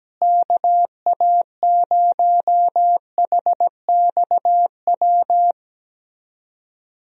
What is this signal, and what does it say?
Morse code, heard as a keyed tone, about 17 wpm, 700 Hz. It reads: KA0HXW
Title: USA Call Signs - 6 Characters - Repeated 2x - 17wpm